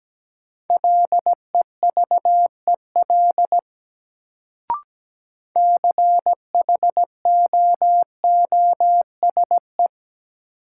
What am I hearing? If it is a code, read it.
Morse code, heard as a keyed tone, about 17 wpm, 700 Hz. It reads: LEVEL CHOOSE
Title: Sentences from Top 700 Words - New Word Drill - 17wpm